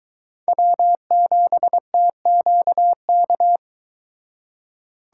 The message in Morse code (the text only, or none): W7TQK